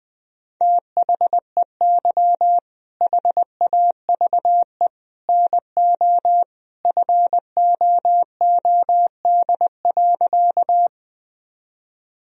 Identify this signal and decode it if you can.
Morse code, heard as a keyed tone, about 20 wpm, 700 Hz. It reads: THEY HAVE NO FOOD.
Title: Sentences from Top 1000 Words - Review Part 1 - 20wpm